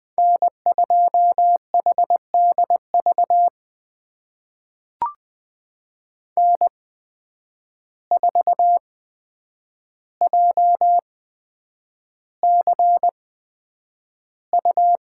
Morse code, heard as a keyed tone, about 20 wpm, spaced as for 9 wpm, 700 Hz. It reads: N2HDV N4JCU
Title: USA Call Signs - 5 Characters - 8x Character Spacing - 20wpm